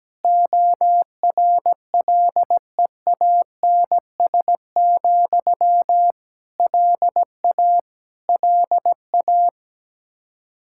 Morse code, heard as a keyed tone, about 17 wpm, 700 Hz. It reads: ORLEANS, LA LA
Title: Practice QSO - Exchange 1 - Beginning - 17wpm